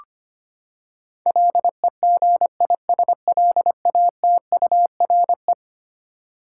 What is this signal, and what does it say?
Morse code, heard as a keyed tone, about 25 wpm, 700 Hz. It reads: LEGISLATURE
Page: morse-code-ninja-practice-25wpm